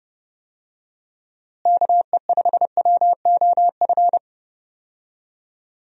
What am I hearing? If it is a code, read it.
Morse code, heard as a keyed tone, about 31 wpm, 700 Hz. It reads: KE5WOF
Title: USA Call Signs - 6 Characters - Repeated 3x - 30wpm